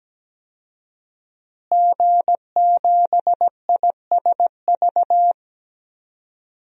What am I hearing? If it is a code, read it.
Morse code, heard as a keyed tone, about 17 wpm, 700 Hz. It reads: G7ISV
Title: DXCC-100 Call Signs - 5 Characters - Repeated 2x - 17wpm